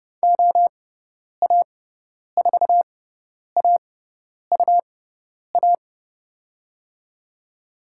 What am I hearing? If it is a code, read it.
Morse code, heard as a keyed tone, about 30 wpm, spaced as for 10 wpm, 700 Hz. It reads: OA4AUA
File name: Lesson 18g - Letter m - Call Signs 30wpm